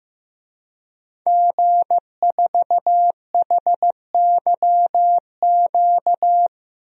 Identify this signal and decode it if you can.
Morse code, heard as a keyed tone, about 15 wpm, 700 Hz. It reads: G4HYQ
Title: DXCC-100 Call Signs - 5 Characters - Repeated 2x - 15wpm